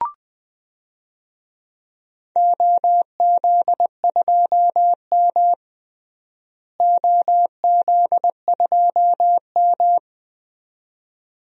Morse code, heard as a keyed tone, about 20 wpm, 700 Hz. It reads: OZ2M OZ2M
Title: DXCC-100 Call Signs - 4 Characters - Repeated 2x - 20wpm